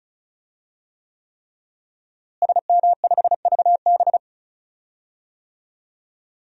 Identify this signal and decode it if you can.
Morse code, heard as a keyed tone, about 36 wpm, 700 Hz. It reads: SM5VB
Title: DXCC-100 Call Signs - 5 Characters - Repeated 3x - 35wpm